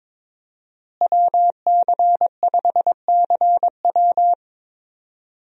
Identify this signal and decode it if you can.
Morse code, heard as a keyed tone, about 22 wpm, 700 Hz. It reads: WC5CW
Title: USA Call Signs - 5 Characters - Repeated 3x - 22wpm